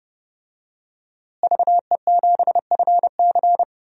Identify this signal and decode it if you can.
Morse code, heard as a keyed tone, about 32 wpm, 700 Hz. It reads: VE7FC
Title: DXCC-100 Call Signs - 5 Characters - Repeated 2x - 30wpm